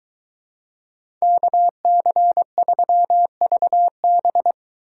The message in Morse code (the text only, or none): KC3VB